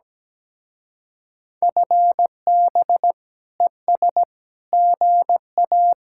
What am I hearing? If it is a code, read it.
Morse code, heard as a keyed tone, about 17 wpm, 700 Hz. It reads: FB ES GA
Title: Speed-Racing Common QSO Phrases - 17wpm